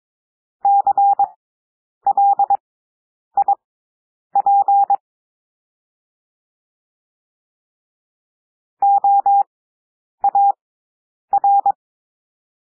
Morse code, heard as a keyed tone, about 22 wpm, spaced as for 9 wpm, 800 Hz. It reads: CLIP OAR
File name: cwmp3 / W2 SOUND WORDS 22 wpm